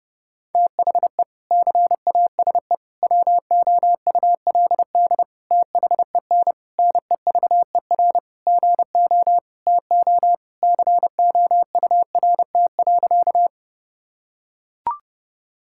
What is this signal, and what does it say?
Morse code, heard as a keyed tone, about 30 wpm, 700 Hz. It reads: THE CASE WOULD THEN NEVER GO TO COURT.
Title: Sentences from Top 1600 Words 30wpm